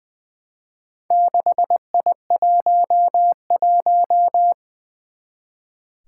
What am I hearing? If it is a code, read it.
Morse code, heard as a keyed tone, about 20 wpm, 700 Hz. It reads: 6I11
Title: Four Challenging Letter-Number Combinations 20wpm